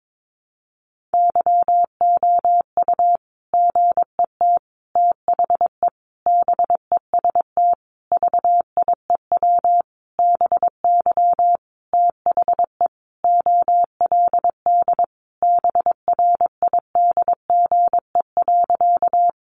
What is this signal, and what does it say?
Morse code, heard as a keyed tone, about 22 wpm, 700 Hz. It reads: YOU GET THE BEST VIEW BY THE OLD BRIDGE.
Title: Sentences from Top 1200 Words 22wpm